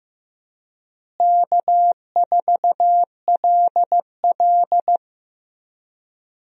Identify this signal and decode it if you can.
Morse code, heard as a keyed tone, about 15 wpm, 700 Hz. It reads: K4LL
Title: USA Call Signs - 4 Characters - Repeated 3x - 15wpm